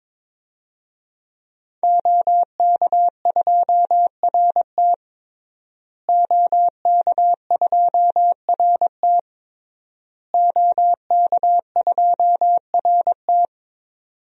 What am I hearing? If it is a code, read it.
Morse code, heard as a keyed tone, about 22 wpm, 700 Hz. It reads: OK2RT OK2RT OK2RT
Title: DXCC-100 Call Signs - 5 Characters - Repeated 3x - 22wpm